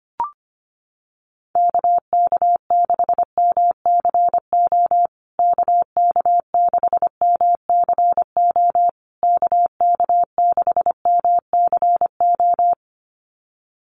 Morse code, heard as a keyed tone, about 25 wpm, 700 Hz. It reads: KK6MCO KK6MCO KK6MCO
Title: USA Call Signs Repeated 3x 25wpm